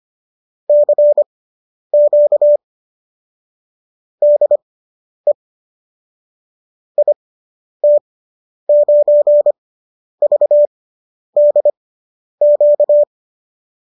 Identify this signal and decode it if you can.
Morse code, heard as a keyed tone, about 25 wpm, spaced as for 10 wpm, 590 Hz. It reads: CQ DE IT9VDQ